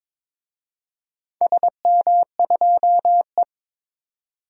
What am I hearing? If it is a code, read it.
Morse code, heard as a keyed tone, about 22 wpm, 700 Hz. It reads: SM2E
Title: DXCC-100 Call Signs - 4 Characters - Repeated 2x - 22wpm